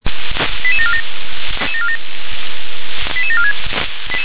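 Five-tone selective call (ZVEI), heard as quick runs of five tones, tones of about 70 ms. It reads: 88747 88746
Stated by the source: ZVEI five-tone call 88912 received by SDR.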